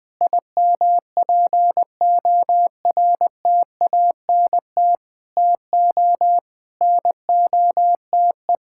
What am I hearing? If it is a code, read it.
Morse code, heard as a keyed tone, about 20 wpm, 700 Hz. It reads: IMPORTANT TO NOTE
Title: Sentences with 8-Word Phrases - 20wpm